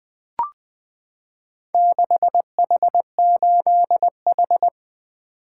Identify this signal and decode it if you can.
Morse code, heard as a keyed tone, about 20 wpm, 700 Hz. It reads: 6H8H
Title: Four Letter-Number Combinations - 20wpm